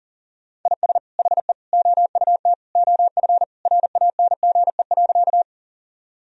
Morse code, heard as a keyed tone, about 40 wpm, 700 Hz. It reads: IS HE OUT OF RANGE.
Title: Sentences from Top 1000 Words - Review Part 4 - 40wpm